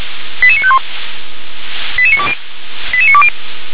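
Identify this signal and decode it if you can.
Five-tone selective call (ZVEI), heard as quick runs of five tones, tones of about 70 ms. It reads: 88951 88920 88920
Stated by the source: ZVEI five-tone selective call received by SDR.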